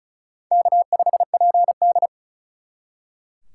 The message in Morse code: K5PD

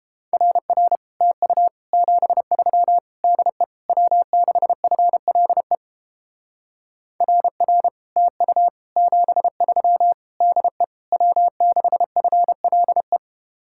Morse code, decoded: RR TU 73 DE W6FLE RR TU 73 DE W6FLE

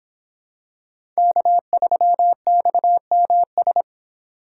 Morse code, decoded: K3XMS

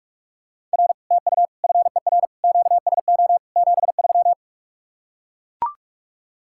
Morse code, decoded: R TU FER QSO 73

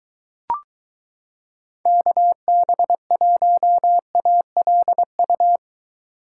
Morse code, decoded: KB1ALU

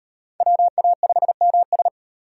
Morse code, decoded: WA5MS

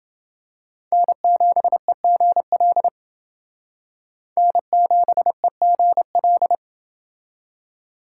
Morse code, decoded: N7EGL N7EGL

